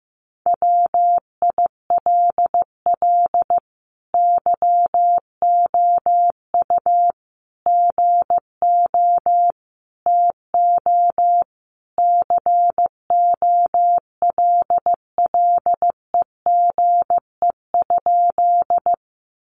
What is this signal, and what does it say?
Morse code, heard as a keyed tone, about 15 wpm, 700 Hz. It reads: WILL YOU GO TO COLLEGE?
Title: Sentences from Top 1300 Words 15wpm